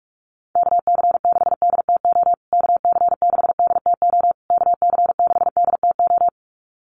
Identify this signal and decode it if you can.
Morse code, heard as a keyed tone, about 44 wpm, 700 Hz. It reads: KC6DTO KC6DTO KC6DTO
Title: USA Call Signs Repeated 3x 45wpm